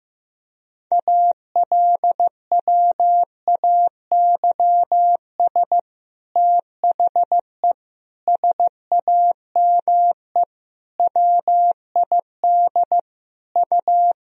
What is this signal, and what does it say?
Morse code, heard as a keyed tone, about 15 wpm, 700 Hz. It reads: ALWAYS THE SAME WID U
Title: Sentences from Top 300 Words and Abbreviations 15wpm